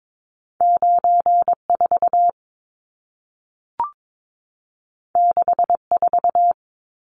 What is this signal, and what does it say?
Morse code, heard as a keyed tone, about 22 wpm, 700 Hz. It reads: 94 64
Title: Numbers - 2-Digit 22wpm